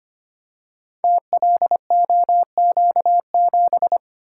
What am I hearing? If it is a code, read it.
Morse code, heard as a keyed tone, about 25 wpm, 700 Hz. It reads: TLOQ7
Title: Code Groups - 5 Letter:Numbers - 25wpm